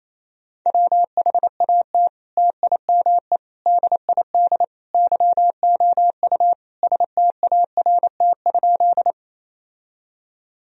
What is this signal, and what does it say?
Morse code, heard as a keyed tone, about 28 wpm, 700 Hz. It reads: WHAT TIME DID YOU START?
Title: Sentences from Top 1000 Words - Review Part 1 - 28wpm